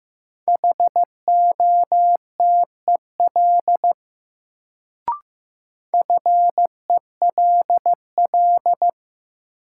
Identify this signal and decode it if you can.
Morse code, heard as a keyed tone, about 15 wpm, 700 Hz. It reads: HOTEL FELL
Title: Sentences from Top 1600 Words - New Word Drill Alt - 15wpm